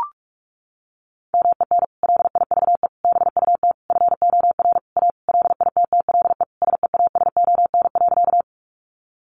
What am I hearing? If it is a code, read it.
Morse code, heard as a keyed tone, about 44 wpm, 700 Hz. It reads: MEN LIVE BUT FOR A LITTLE SEASON.
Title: Sentences from Top 1400 Words 45wpm